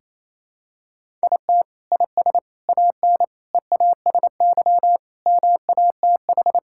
Morse code, heard as a keyed tone, about 28 wpm, 700 Hz. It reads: IT IS AN EASY MATH
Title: Sentences from Top 1000 Words - Review Part 6 - 28wpm